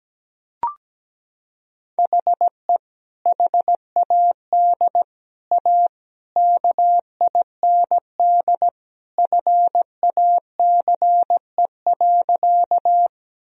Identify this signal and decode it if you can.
Morse code, heard as a keyed tone, about 17 wpm, 700 Hz. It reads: HE HAD A KIND FACE.
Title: Sentences from Top 1000 Words - Review Part 2 - 17wpm